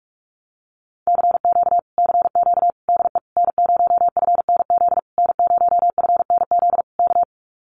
Morse code, heard as a keyed tone, about 44 wpm, 700 Hz. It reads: CQ CQ DE N0FNZ N0FNZ K
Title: Common QSO Phrases Encore 45wpm